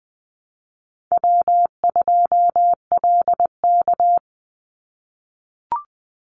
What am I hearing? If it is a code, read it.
Morse code, heard as a keyed tone, about 20 wpm, 700 Hz. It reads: W2LK
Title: Easy Call Signs - 20wpm